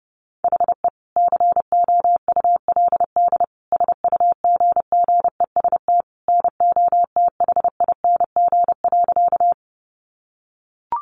Morse code, heard as a keyed tone, about 30 wpm, 700 Hz. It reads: HE COULD SUGGEST NOTHING.
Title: Sentences from Top 1300 Words 30wpm